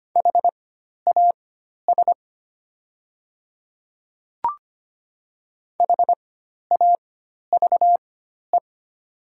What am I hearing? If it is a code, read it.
Morse code, heard as a keyed tone, about 25 wpm, spaced as for 12 wpm, 700 Hz. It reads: HAS HAVE
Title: 4x Character Spacing - Top 200 Words - 25wpm